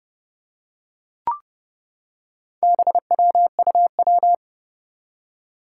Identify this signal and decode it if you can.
Morse code, heard as a keyed tone, about 30 wpm, 700 Hz. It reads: BWUW
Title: Four Challenging Letter-Number Combinations 30wpm